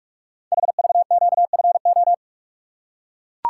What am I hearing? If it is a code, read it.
Morse code, heard as a keyed tone, about 44 wpm, 700 Hz. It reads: HVQFO
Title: Code Groups - 5 Letters - 45wpm